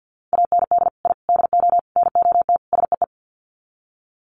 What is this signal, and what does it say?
Morse code, heard as a keyed tone, about 48 wpm, 700 Hz. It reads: AND I DO NOT SEE
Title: Sets of 5 Words Encore2 50wpm